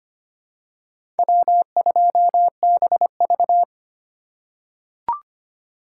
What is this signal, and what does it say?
Morse code, heard as a keyed tone, about 25 wpm, 700 Hz. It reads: W2BV
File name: practice / 25wpm / Four Letter-Number Combinations - 25wpm